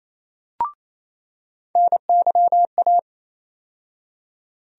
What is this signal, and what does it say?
Morse code, heard as a keyed tone, about 28 wpm, 700 Hz. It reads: NYA
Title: USA Call Sign Postfixes - 3 Letters 28wpm